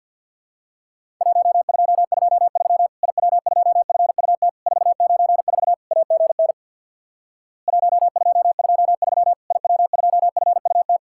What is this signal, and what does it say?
Morse code, heard as a keyed tone, about 50 wpm, 700 Hz. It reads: 1223 IW2FUT 494 AGN 1223 IW2FUT
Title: Worked All Europe DX Contest - QTC Exchange with Repeats - 50wpm